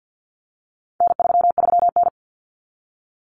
Morse code, heard as a keyed tone, about 50 wpm, 700 Hz. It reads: N33N
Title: Common Cut Numbers - 4-Digit 50wpm